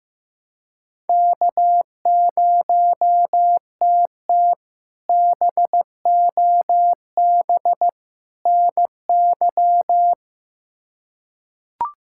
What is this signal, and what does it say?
Morse code, heard as a keyed tone, about 15 wpm, 700 Hz. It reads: K0TT BOB NY
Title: K1USN SST QSO Practice - 15wpm